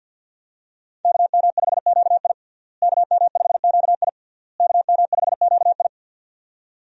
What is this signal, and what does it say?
Morse code, heard as a keyed tone, about 47 wpm, 700 Hz. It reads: KM5QI KM5QI KM5QI
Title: USA Call Signs - 5 Characters - Repeated 3x - 50wpm